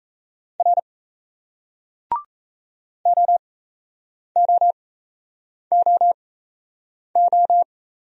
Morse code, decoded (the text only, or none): R OOOO